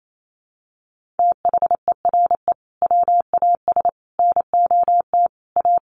Morse code, heard as a keyed tone, about 28 wpm, 700 Hz. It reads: THERE WAS NOT A
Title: Sentences from Top 1100 Words 28wpm